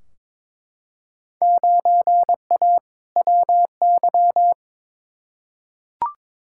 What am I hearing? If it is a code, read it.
Morse code, heard as a keyed tone, about 22 wpm, 700 Hz. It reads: 9A WY
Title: Field Day Exchanges Encore - 22wpm